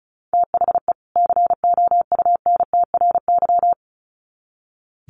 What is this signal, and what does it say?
Morse code, heard as a keyed tone, about 34 wpm, 700 Hz. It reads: THE COUNTRY
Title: Sets of 2 Words 35wpm